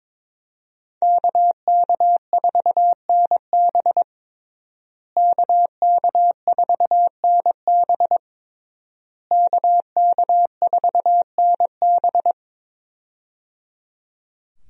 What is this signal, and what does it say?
Morse code, heard as a keyed tone, about 22 wpm, 700 Hz. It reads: KK4NB KK4NB KK4NB
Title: USA Call Signs - 5 Characters - Repeated 3x - 22wpm